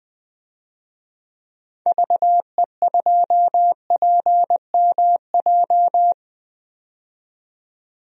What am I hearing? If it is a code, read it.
Morse code, heard as a keyed tone, about 20 wpm, 700 Hz. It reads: VE2PMJ
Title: DXCC-100 Call Signs - 6 Characters - 20wpm